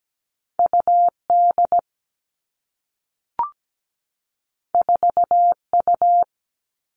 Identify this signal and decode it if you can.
Morse code, heard as a keyed tone, about 17 wpm, 700 Hz. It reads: UD 4U
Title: All Cut Numbers - 2-Digit 17wpm